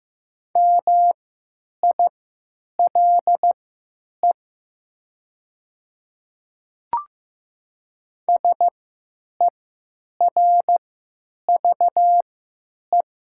Morse code, heard as a keyed tone, about 15 wpm, spaced as for 8 wpm, 700 Hz. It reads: MILE SERVE
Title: 3x Character Spacing - Top 500 Words - 15wpm